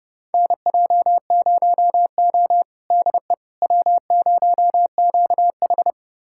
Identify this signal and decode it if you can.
Morse code, heard as a keyed tone, about 30 wpm, 700 Hz. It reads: NJ0O DE W0QH
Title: USA Callsign DE Callsign - 4 Characters - 30wpm